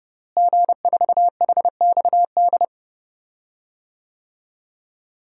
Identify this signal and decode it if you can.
Morse code, heard as a keyed tone, about 32 wpm, 700 Hz. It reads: G4HXD
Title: DXCC-100 Call Signs - 5 Characters - Repeated 2x - 30wpm